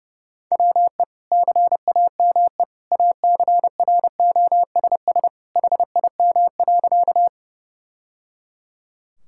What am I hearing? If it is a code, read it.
Morse code, heard as a keyed tone, about 30 wpm, 700 Hz. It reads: WE CAME ACROSS HIM.
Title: Sentences from Top 1000 Words - Review Part 5 - 30wpm